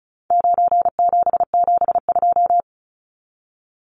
Morse code, heard as a keyed tone, about 34 wpm, 700 Hz. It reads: 9772
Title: Numbers - 4-Digit 35wpm